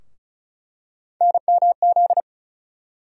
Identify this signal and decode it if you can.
Morse code, heard as a keyed tone, about 34 wpm, 700 Hz. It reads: NMZ
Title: USA Call Sign Postfixes - 3 Letters 35wpm